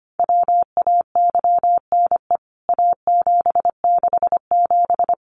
Morse code, heard as a keyed tone, about 25 wpm, 700 Hz. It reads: WAYNE A767
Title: CWT Exchange 25wpm